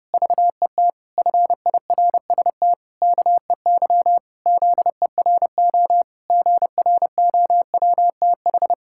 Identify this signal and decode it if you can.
Morse code, heard as a keyed tone, about 30 wpm, 700 Hz. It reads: VET FIRST KEY ZERO GROWTH
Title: Pangram Challenge v2 - 30wpm